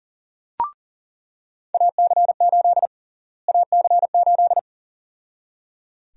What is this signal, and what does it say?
Morse code, heard as a keyed tone, about 38 wpm, 700 Hz. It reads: AC8 AC8
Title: USA Call Sign Prefixes - Repeated 2x - 40wpm